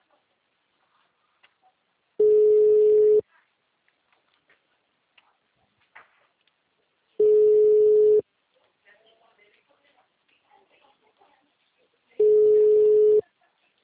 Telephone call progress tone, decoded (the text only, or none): ringback tone